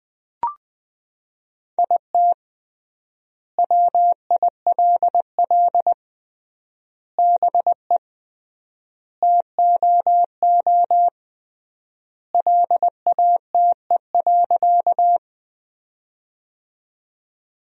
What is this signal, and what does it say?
Morse code, heard as a keyed tone, about 20 wpm, 700 Hz. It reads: IT WILL BE TOO LATE.
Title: 3x Word-Spacing - Sentences from Top 300 Words - 20wpm